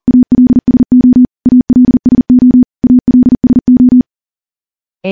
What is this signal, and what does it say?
Morse code, heard as a keyed tone, about 40 wpm, 250 Hz. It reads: ALSO ALSO ALSO
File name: morse-training-40WPM